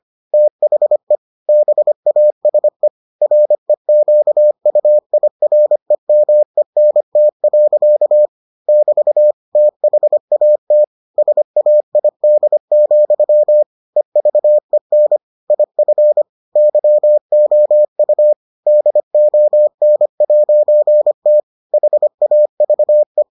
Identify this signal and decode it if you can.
Morse code, heard as a keyed tone, about 25 wpm, 600 Hz. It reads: THE BASE REQUIREMENT. <BT> THAT SAID, EVEN IF YOU DON'T HAVE